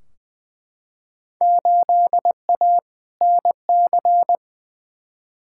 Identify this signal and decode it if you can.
Morse code, heard as a keyed tone, about 20 wpm, 700 Hz. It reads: 8A NC
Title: Field Day Exchanges - 20wpm